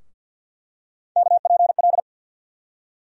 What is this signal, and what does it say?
Morse code, heard as a keyed tone, about 52 wpm, 700 Hz. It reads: KPL
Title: USA Call Sign Postfixes - 3 Letters 50wpm